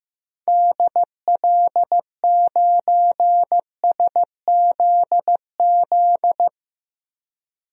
Morse code, heard as a keyed tone, about 15 wpm, 700 Hz. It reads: DL9SZZ
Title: DXCC-100 Call Signs - 6 Characters - Repeated 2x - 15wpm